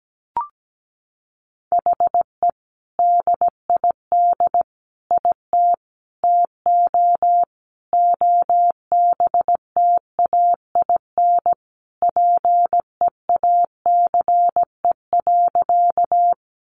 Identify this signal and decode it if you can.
Morse code, heard as a keyed tone, about 17 wpm, 700 Hz. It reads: HE DID IT TO OBTAIN PEACE.
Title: Sentences from Top 1500 Words 17wpm